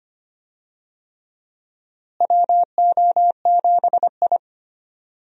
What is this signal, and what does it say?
Morse code, heard as a keyed tone, about 25 wpm, 700 Hz. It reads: WO7I
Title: USA Call Signs - 4 Characters - Repeated 3x - 25wpm